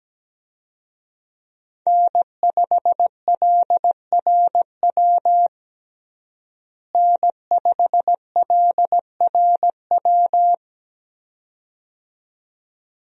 Morse code, heard as a keyed tone, about 17 wpm, 700 Hz. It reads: N5LRW N5LRW
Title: USA Call Signs - 5 Characters - Repeated 2x - 17wpm